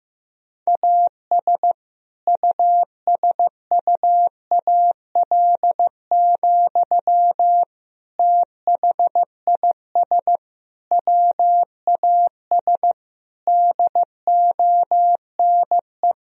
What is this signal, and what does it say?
Morse code, heard as a keyed tone, about 15 wpm, 700 Hz. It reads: AS USUAL, THIS WAS DONE
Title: Sentences from Top 1000 Words - Review Part 2 - 15wpm